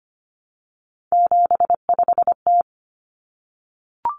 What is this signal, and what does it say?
Morse code, heard as a keyed tone, about 25 wpm, 700 Hz. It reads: 75T
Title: Common Cut Numbers - 3-Digit 25wpm